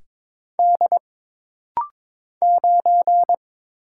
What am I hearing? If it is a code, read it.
Morse code, heard as a keyed tone, about 22 wpm, 700 Hz. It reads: D 9
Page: This rapid-fire timing is geared to push your ICR (Instant Character Recognition) skills.